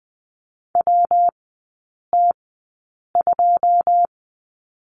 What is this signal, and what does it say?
Morse code, heard as a keyed tone, about 20 wpm, spaced as for 8 wpm, 700 Hz. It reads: WT2
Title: 114 Lesson 19e - Letter w - Three Characters 20wpm